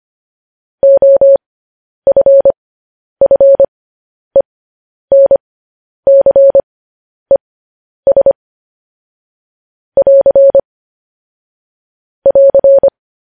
Morse code, heard as a keyed tone, about 25 wpm, spaced as for 10 wpm, 550 Hz. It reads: OFFENCES <AR> <AR>
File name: podcast / News headlines-1x-25x10